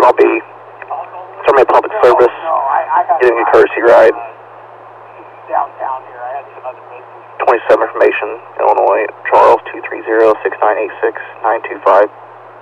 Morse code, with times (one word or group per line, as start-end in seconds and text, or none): none